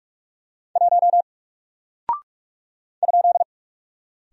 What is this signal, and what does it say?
Morse code, heard as a keyed tone, about 47 wpm, 700 Hz. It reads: 1 ?
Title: Single Character (Letter:Number:Punctuation:Prosign) 45wpm